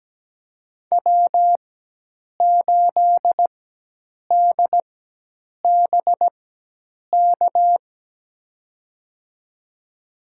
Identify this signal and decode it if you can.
Morse code, heard as a keyed tone, about 17 wpm, spaced as for 8 wpm, 700 Hz. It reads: W8DBK
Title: USA Call Signs - 5 Characters - 4x Character Spacing - 17wpm